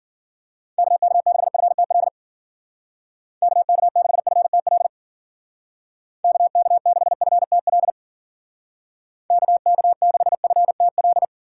Morse code, heard as a keyed tone, about 50 wpm, 700 Hz. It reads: KK6FTL KK6FTL KK6FTL KK6FTL
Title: Speed Racing - USA 2x3 Call Signs 40wpm